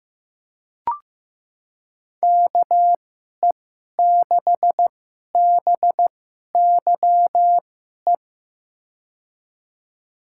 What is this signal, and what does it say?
Morse code, heard as a keyed tone, about 15 wpm, spaced as for 11 wpm, 700 Hz. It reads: KE6BYE
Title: USA Call Signs - 6 Characters - 2x Character Spacing - 15wpm